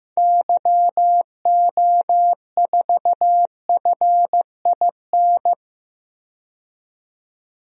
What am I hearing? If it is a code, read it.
Morse code, heard as a keyed tone, about 15 wpm, 700 Hz. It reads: YO4FIN
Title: DXCC-100 Call Signs - 6 Characters - 15wpm